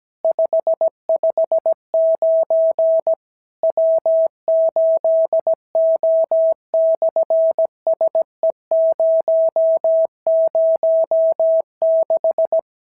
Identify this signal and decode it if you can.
Morse code, heard as a keyed tone, about 17 wpm, 650 Hz. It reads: 559 W8O/SE006
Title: SOTA QSO Practice 17wpm